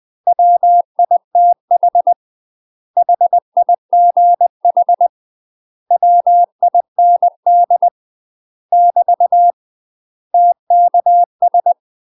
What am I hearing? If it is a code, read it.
Morse code, heard as a keyed tone, about 20 wpm, 700 Hz. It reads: WITH HIGH WIND <BT> TKS
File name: UK MASTER QSOs TRAINING / 20WPM / UK MASTER QSOs 0065